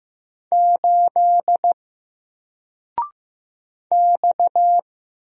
Morse code, heard as a keyed tone, about 15 wpm, 700 Hz. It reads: 8 X
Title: Single Letter-Number Alt 15wpm